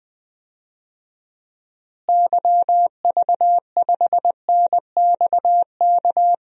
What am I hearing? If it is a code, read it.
Morse code, heard as a keyed tone, about 21 wpm, 700 Hz. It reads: YV5NXK
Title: DXCC-100 Call Signs - 6 Characters - Repeated 2x - 20wpm